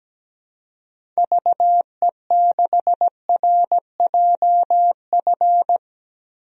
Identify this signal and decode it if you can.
Morse code, heard as a keyed tone, about 17 wpm, 700 Hz. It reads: VE6RJF
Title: DXCC-100 Call Signs - 6 Characters - 17wpm